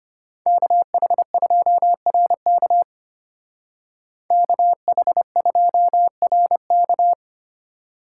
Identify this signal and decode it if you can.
Morse code, heard as a keyed tone, about 30 wpm, 700 Hz. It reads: KH2RK KH2RK
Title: Speed Racing - USA 2x2 Call Signs 25wpm